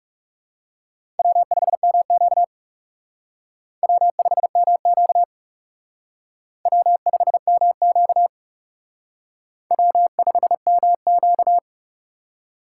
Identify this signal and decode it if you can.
Morse code, heard as a keyed tone, about 36 wpm, 700 Hz. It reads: W5MQ W5MQ W5MQ W5MQ
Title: Speed Racing - USA Call Signs 30wpm